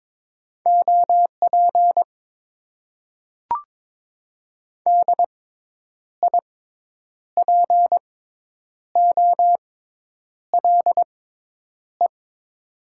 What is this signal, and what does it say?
Morse code, heard as a keyed tone, about 22 wpm, spaced as for 9 wpm, 700 Hz. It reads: OP DIPOLE
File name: Top 90 QSO Elements - 6x Character Spacing - 22wpm